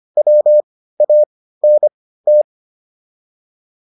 Morse code, heard as a keyed tone, about 25 wpm, spaced as for 15 wpm, 600 Hz. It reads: WANT
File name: WD305 15